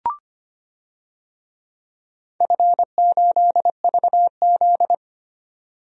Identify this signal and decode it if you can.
Morse code, heard as a keyed tone, about 25 wpm, 700 Hz. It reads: F8VZ
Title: DXCC-100 Call Signs - 4 Characters - 25wpm